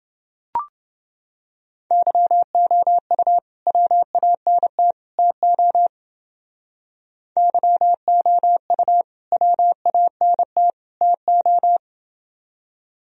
Morse code, heard as a keyed tone, about 28 wpm, 700 Hz. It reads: YOU WANT TO YOU WANT TO